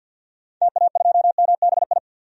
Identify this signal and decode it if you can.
Morse code, heard as a keyed tone, about 50 wpm, 700 Hz. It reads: TA2MBI
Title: DXCC-100 Call Signs - 6 Characters - 50wpm